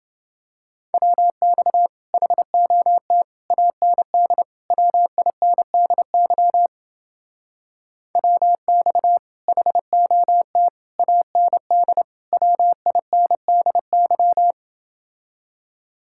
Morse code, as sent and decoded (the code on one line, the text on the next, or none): .-- -..- / .... --- - / .- -. -.. / .-- .. -. -.. -.-- / .-- -..- / .... --- - / .- -. -.. / .-- .. -. -.. -.--
WX HOT AND WINDY WX HOT AND WINDY